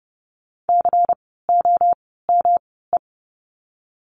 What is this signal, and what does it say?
Morse code, heard as a keyed tone, about 30 wpm, spaced as for 17 wpm, 700 Hz. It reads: COME
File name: course / 30:10wpm / 115 Lesson 19f - Letter w - Words 30wpm